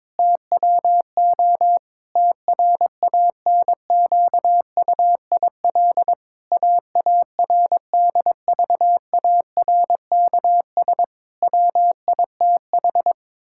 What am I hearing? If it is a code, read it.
Morse code, heard as a keyed tone, about 22 wpm, 700 Hz. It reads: TWO TRANQUIL AARDVARKS WITH
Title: Pangram Challenge v2 - 22wpm